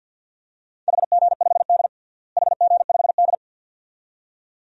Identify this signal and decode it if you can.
Morse code, heard as a keyed tone, about 51 wpm, 700 Hz. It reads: HG5D HG5D
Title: DXCC-100 Call Signs - 4 Characters - Repeated 2x - 50wpm